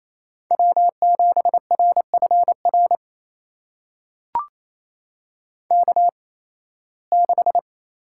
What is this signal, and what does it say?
Morse code, heard as a keyed tone, about 28 wpm, spaced as for 18 wpm, 700 Hz. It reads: W7RFR K6
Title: USA Call Signs - 5 Characters - 8x Character Spacing - 28wpm